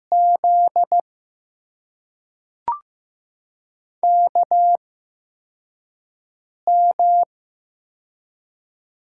Morse code, heard as a keyed tone, about 15 wpm, spaced as for 4 wpm, 700 Hz. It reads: Z KM
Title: USA Call Signs - 6 Characters - 8x Character Spacing - 15wpm